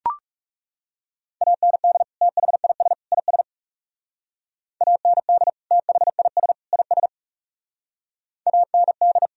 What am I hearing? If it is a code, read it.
Morse code, heard as a keyed tone, about 39 wpm, 700 Hz. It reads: AND THIS IS AND THIS IS AND